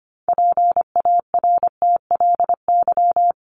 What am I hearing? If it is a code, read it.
Morse code, heard as a keyed tone, about 25 wpm, 700 Hz. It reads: PARTLY
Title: LY Suffix - Top 100 Words with Base Words 25wpm